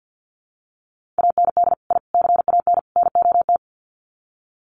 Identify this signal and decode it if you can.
Morse code, heard as a keyed tone, about 48 wpm, 700 Hz. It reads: AND I CAN NOT
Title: Sets of 4 Words 50wpm